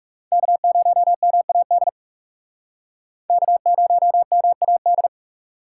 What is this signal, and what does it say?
Morse code, heard as a keyed tone, about 37 wpm, 700 Hz. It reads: K0MAD K0MAD
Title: Speed Racing - USA 1x3 Call Signs 30wpm